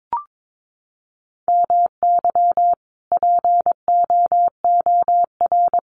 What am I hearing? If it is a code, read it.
Morse code, heard as a keyed tone, about 22 wpm, 700 Hz. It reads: MY POOR
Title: Sentences from Top 1000 Words 22wpm